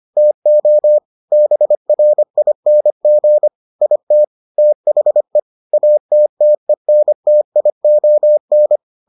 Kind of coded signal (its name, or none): Morse code